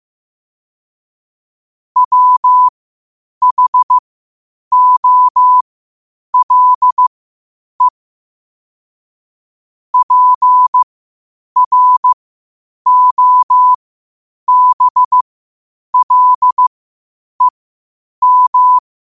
Morse code, heard as a keyed tone, about 15 wpm, spaced as for 8 wpm, 1000 Hz. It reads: WHOLE PROBLEM
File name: qotdfiles / 8wpm